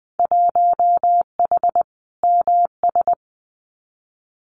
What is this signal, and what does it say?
Morse code, heard as a keyed tone, about 20 wpm, 700 Hz. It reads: 1H MS
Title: Winter Field Day Exchanges 20wpm